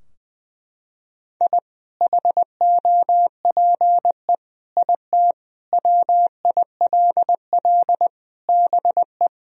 Morse code, heard as a keyed tone, about 20 wpm, 700 Hz. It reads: I HOPE IT WILL BE
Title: Sentences from Top 1000 Words - Review Part 2 - 20wpm